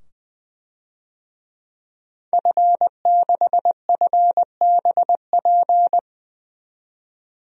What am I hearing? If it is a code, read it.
Morse code, heard as a keyed tone, about 20 wpm, 700 Hz. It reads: F6FBP
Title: DXCC-100 Call Signs - 5 Characters - 20wpm